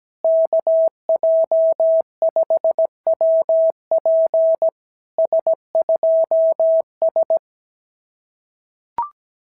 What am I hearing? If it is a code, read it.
Morse code, heard as a keyed tone, about 16 wpm, 650 Hz. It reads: KJ5WP S2S
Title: SOTA QSO Practice 17wpm